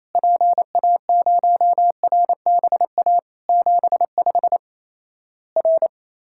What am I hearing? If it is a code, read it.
Morse code, heard as a keyed tone, about 28 wpm, 700 Hz. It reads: PA0RBA 75 R
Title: Worked All Europe DX Contest - QTC Exchange - 28wpm